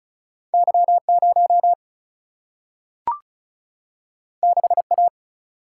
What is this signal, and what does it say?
Morse code, heard as a keyed tone, about 36 wpm, 700 Hz. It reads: Y 0 6 A
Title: Single-Letter Single-Number 35wpm